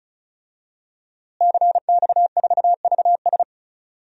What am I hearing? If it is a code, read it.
Morse code, heard as a keyed tone, about 36 wpm, 700 Hz. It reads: CX4VS
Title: DXCC-100 Call Signs - 5 Characters - Repeated 3x - 35wpm